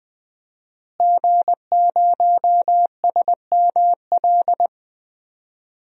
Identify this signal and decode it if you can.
Morse code, heard as a keyed tone, about 20 wpm, 700 Hz. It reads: G0SML
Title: DXCC-100 Call Signs - 5 Characters - Repeated 2x - 20wpm